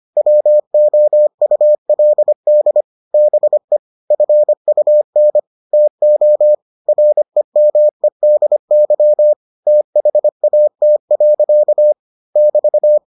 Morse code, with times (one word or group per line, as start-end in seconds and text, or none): 0.2-2.8 WOULD
3.1-3.8 BE
4.1-5.4 FUN
5.7-6.5 TO
6.9-9.3 REMEDY
9.7-11.9 THAT.
12.4-13.0 <BT>